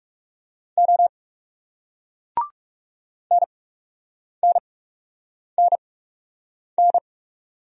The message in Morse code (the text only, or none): O NNNN